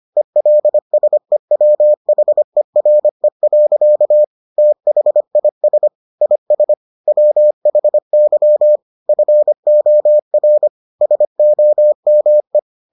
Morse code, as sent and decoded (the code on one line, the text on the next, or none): . .-.. ... .-- .... .-. .-.-.- / - .... .. ... / .. ... / .-- .... -.-- / ..-. --- .-. / ... --- -- .
ELSEWHERE. THIS IS WHY FOR SOME